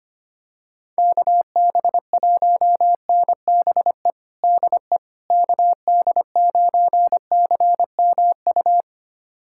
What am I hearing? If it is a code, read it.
Morse code, heard as a keyed tone, about 25 wpm, 700 Hz. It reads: KB1NBE DE KD9CMU
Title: USA Callsign DE Callsign - 6 Characters - 25wpm